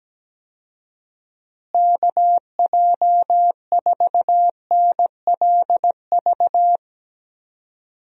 Morse code, heard as a keyed tone, about 17 wpm, 700 Hz. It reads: KJ4NLV
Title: USA Call Signs - 6 Characters - Repeated 2x - 17wpm